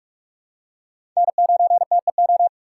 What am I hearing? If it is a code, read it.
Morse code, heard as a keyed tone, about 46 wpm, 700 Hz. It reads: N9TEO
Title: USA Call Signs - 4x Character Spacing - 45wpm